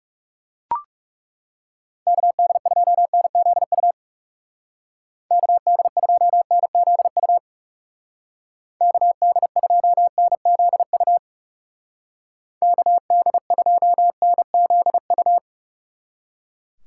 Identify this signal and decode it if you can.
Morse code, heard as a keyed tone, about 36 wpm, 700 Hz. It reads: KD2NZU KD2NZU KD2NZU KD2NZU
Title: Speed Racing - USA 2x3 Call Signs 30wpm